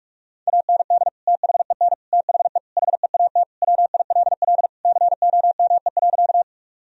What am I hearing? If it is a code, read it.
Morse code, heard as a keyed tone, about 46 wpm, 700 Hz. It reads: AND THEN THE HEAT WILL COME.
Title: Sentences from Top 1000 Words - Review Part 3 - 45wpm